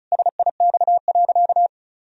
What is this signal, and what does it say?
Morse code, heard as a keyed tone, about 36 wpm, 700 Hz. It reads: SIX.
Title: Sentences from Top 1000 Words - Review Part 2 - 35wpm